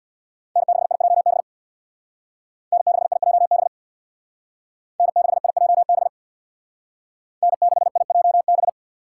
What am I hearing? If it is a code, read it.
Morse code, heard as a keyed tone, about 63 wpm, 700 Hz. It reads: N6IJB N6IJB N6IJB N6IJB